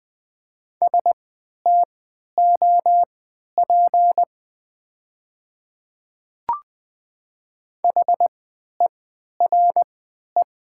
Morse code, heard as a keyed tone, about 20 wpm, spaced as for 11 wpm, 700 Hz. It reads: STOP HERE
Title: 3x Character Spacing - Top 300 Words - 20wpm